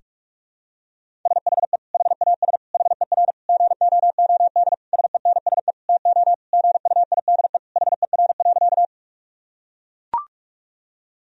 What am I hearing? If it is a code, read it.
Morse code, heard as a keyed tone, about 42 wpm, 700 Hz. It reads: SHE HAS HER GOOD SENSE TO GUIDE HER.
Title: Sentences from Top 1000 Words - Review Part 4 - 45wpm